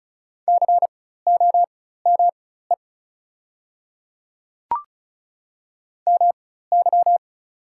Morse code, heard as a keyed tone, about 36 wpm, spaced as for 16 wpm, 690 Hz. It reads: COME MY
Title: 4x Character Spacing - Top 100 Words - 35wpm